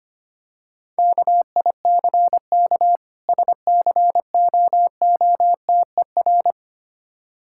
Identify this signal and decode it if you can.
Morse code, heard as a keyed tone, about 25 wpm, 700 Hz. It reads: KICK SCOOTER
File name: practice / 25wpm / Difficult Words - Speed-Racing with Hints - 25wpm